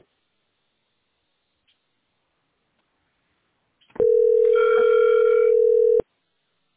Ringback tone (North America), 440 + 480 Hz.